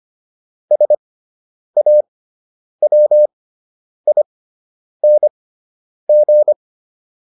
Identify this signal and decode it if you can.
Morse code, heard as a keyed tone, about 25 wpm, spaced as for 9 wpm, 600 Hz. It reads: SAWING
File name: ING6 09